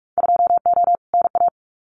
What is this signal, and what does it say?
Morse code, heard as a keyed tone, about 46 wpm, 700 Hz. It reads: 2O NA
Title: Winter Field Day Exchanges 45wpm